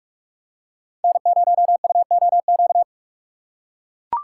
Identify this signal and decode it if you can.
Morse code, heard as a keyed tone, about 42 wpm, 700 Hz. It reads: N0UOQ
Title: USA Call Signs - 6x Character Spacing - 45wpm